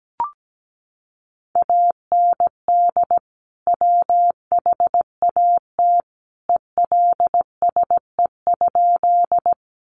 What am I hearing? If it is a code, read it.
Morse code, heard as a keyed tone, about 17 wpm, 700 Hz. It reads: AND WHAT ELSE?
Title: Sentences from Top 800 Words 17wpm